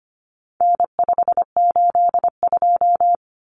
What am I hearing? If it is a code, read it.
Morse code, heard as a keyed tone, about 25 wpm, 700 Hz. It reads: N582
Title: Common Cut Numbers - 4-Digit 25wpm